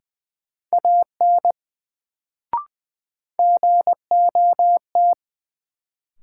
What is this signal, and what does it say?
Morse code, heard as a keyed tone, about 20 wpm, 700 Hz. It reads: AN GOT
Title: Sentences from Top 300 Words - New Word Drill - 20wpm